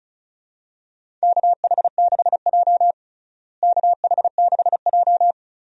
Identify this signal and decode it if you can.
Morse code, heard as a keyed tone, about 35 wpm, 700 Hz. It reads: KH6J KH6J
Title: DXCC-100 Call Signs - 4 Characters - Repeated 2x - 35wpm